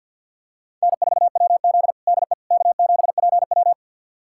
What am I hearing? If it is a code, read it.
Morse code, heard as a keyed tone, about 48 wpm, 700 Hz. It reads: N4WZ DE K7PW
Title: USA Callsign DE Callsign - 4 Characters - 50wpm